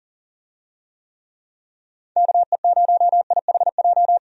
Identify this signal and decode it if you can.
Morse code, heard as a keyed tone, about 40 wpm, 700 Hz. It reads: KE0IHJ
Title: USA Call Signs - 6 Characters - 40wpm